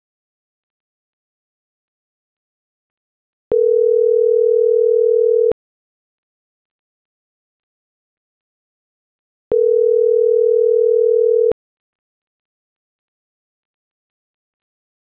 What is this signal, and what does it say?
Ringback tone (North America), 440 + 480 Hz.